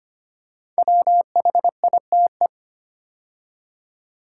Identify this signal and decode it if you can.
Morse code, heard as a keyed tone, about 25 wpm, 700 Hz. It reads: WHITE
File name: 5 Letter Words - First 3 Letters Given - 25wpm